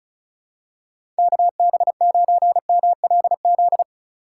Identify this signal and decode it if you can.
Morse code, heard as a keyed tone, about 36 wpm, 700 Hz. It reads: KB9MLZ
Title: USA Call Signs - 6 Characters - 2x Character Spacing - 35wpm